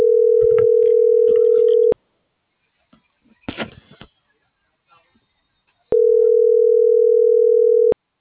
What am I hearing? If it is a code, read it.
Ringback tone (North America), 440 + 480 Hz.